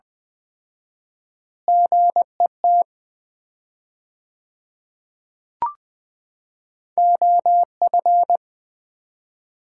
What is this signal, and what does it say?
Morse code, heard as a keyed tone, about 21 wpm, 700 Hz. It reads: GET OF
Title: 4x Word-Spacing - Sets of 3 Words - 20wpm